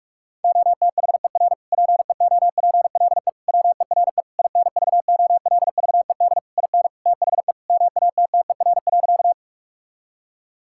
Morse code, heard as a keyed tone, about 46 wpm, 700 Hz. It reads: OTHER PEOPLE WERE INVOLVED IN THE MATTER.
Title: Sentences from Top 1000 Words - Review Part 5 - 45wpm